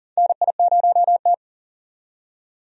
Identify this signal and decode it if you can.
Morse code, heard as a keyed tone, about 40 wpm, 700 Hz. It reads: NI0T
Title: USA Call Signs - 4 Characters - 40wpm